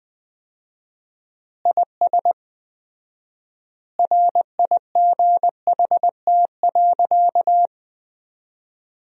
Morse code, heard as a keyed tone, about 21 wpm, 700 Hz. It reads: IS RIGHT.
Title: 4x Word-Spacing - Sentences from Top 300 Words - 20wpm